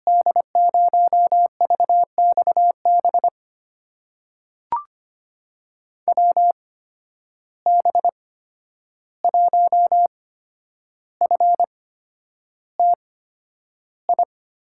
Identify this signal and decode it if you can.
Morse code, heard as a keyed tone, about 25 wpm, spaced as for 10 wpm, 700 Hz. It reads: D0VXB WB1FTI
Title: USA Call Signs - 6 Characters - 8x Character Spacing - 25wpm